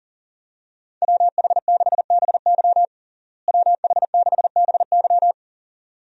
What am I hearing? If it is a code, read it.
Morse code, heard as a keyed tone, about 40 wpm, 700 Hz. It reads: WH6BY WH6BY
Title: USA Call Signs - 5 Characters - Repeated 2x - 40wpm